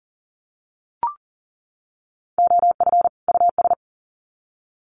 Morse code, heard as a keyed tone, about 40 wpm, 700 Hz. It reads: OF US
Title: Sets of 2 Words 40wpm